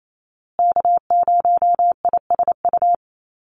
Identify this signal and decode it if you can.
Morse code, heard as a keyed tone, about 28 wpm, 700 Hz. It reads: K0ISU
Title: USA Call Signs Repeated 2x 28wpm